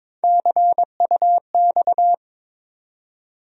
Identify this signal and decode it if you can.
Morse code, heard as a keyed tone, about 22 wpm, 700 Hz. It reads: CUX
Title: USA Call Sign Postfixes - 3 Letters 22wpm